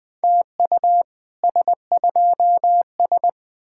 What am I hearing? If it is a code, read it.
Morse code, heard as a keyed tone, about 21 wpm, 700 Hz. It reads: TU S2S